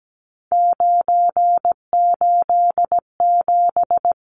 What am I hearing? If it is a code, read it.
Morse code, heard as a keyed tone, about 17 wpm, 700 Hz. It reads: 987
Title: Numbers - 3-Digit 17wpm